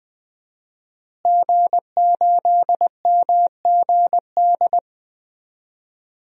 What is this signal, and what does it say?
Morse code, heard as a keyed tone, about 19 wpm, 700 Hz. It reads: G8MGD
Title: DXCC-100 Call Signs - 5 Characters - Repeated 2x - 20wpm